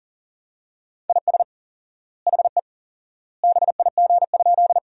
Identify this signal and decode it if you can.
Morse code, heard as a keyed tone, about 37 wpm, 700 Hz. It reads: IS HE BIG?
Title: 4x Word-Spacing - Sentences from Top 200 Words - 40wpm